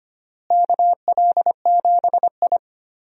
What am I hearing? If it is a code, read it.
Morse code, heard as a keyed tone, about 25 wpm, 700 Hz. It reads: KL7I
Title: DXCC-100 Call Signs - 4 Characters - Repeated 3x - 25wpm